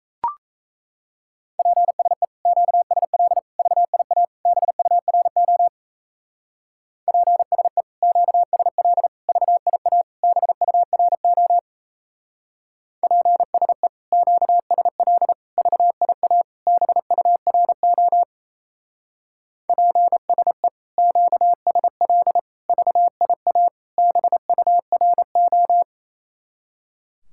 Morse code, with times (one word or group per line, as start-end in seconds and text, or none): 1.6-2.2 PSE
2.5-3.4 QSL
3.6-4.2 VIA
4.5-5.7 BURO
7.1-7.8 PSE
8.0-9.1 QSL
9.3-10.0 VIA
10.2-11.6 BURO
13.0-13.9 PSE
14.1-15.3 QSL
15.6-16.4 VIA
16.7-18.2 BURO
19.7-20.7 PSE
21.0-22.4 QSL
22.7-23.7 VIA
24.0-25.8 BURO